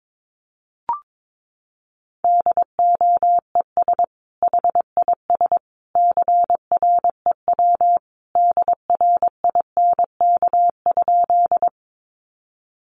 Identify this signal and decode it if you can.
Morse code, heard as a keyed tone, about 22 wpm, 700 Hz. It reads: DOES HIS CREW DRINK?
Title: Sentences from Top 1500 Words 22wpm